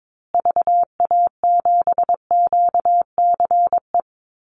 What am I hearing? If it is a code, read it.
Morse code, heard as a keyed tone, about 22 wpm, 700 Hz. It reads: VA7QCE
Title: Call Signs Repeated 3x - 22wpm